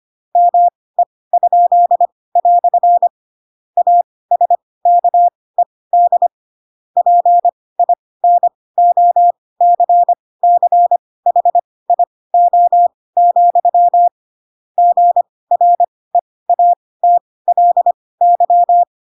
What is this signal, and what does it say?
Morse code, heard as a keyed tone, about 25 wpm, spaced as for 18 wpm, 700 Hz. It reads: ME?" ASKED PINOCCHIO, GREATLY
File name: Pinocchio / Pinocchio ch4 12 25 ik1hge